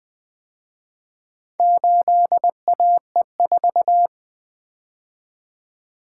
Morse code, heard as a keyed tone, about 24 wpm, 700 Hz. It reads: 8AE4
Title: Speed-Racing - Four Letter-Number Combinations - 20wpm